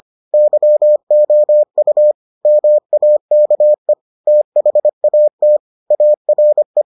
Morse code, heard as a keyed tone, about 25 wpm, 600 Hz. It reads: YOU MAKE THAT ARE